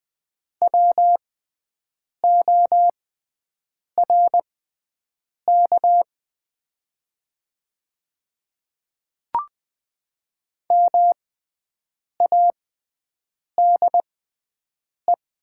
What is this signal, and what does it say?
Morse code, heard as a keyed tone, about 20 wpm, spaced as for 7 wpm, 700 Hz. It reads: WORK MADE
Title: 6x Character Spacing - Top 300 Words - 20wpm